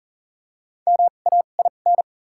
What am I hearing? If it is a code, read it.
Morse code, heard as a keyed tone, about 40 wpm, spaced as for 29 wpm, 700 Hz. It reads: MAIN
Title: 2x Character Spacing - Top 500 Words - 40wpm